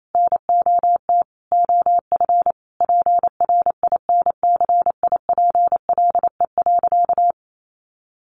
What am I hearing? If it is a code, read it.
Morse code, heard as a keyed tone, about 28 wpm, 700 Hz. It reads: NOT OF PRINCIPLE.
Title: Sentences from Top 1400 Words 28wpm